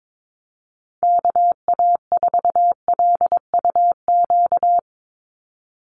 Morse code, heard as a keyed tone, about 22 wpm, 700 Hz. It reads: KA4LUQ
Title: USA Call Signs 22wpm